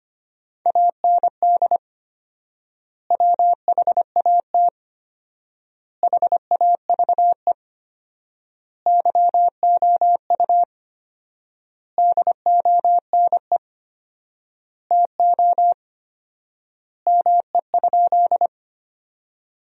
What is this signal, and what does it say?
Morse code, heard as a keyed tone, about 25 wpm, 700 Hz. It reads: AND WHAT HAVE YOU DONE TO ME?
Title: 4x Word-Spacing - Sentences from Top 500 Words - 25wpm